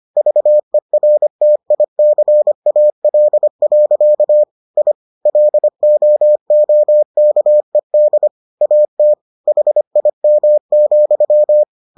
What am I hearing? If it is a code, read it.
Morse code, heard as a keyed tone, about 25 wpm, 600 Hz. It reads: VERTICAL. I LOOKED AT HIM,